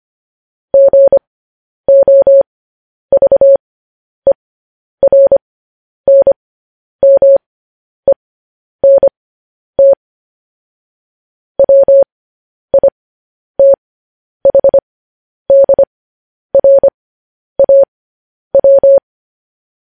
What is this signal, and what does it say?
Morse code, heard as a keyed tone, about 25 wpm, spaced as for 10 wpm, 550 Hz. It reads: GOVERNMENT WITHDRAW